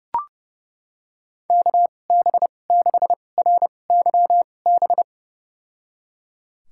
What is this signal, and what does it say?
Morse code, heard as a keyed tone, about 32 wpm, spaced as for 22 wpm, 700 Hz. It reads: KB6RYB
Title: USA Call Signs - 6 Characters - 2x Character Spacing - 30wpm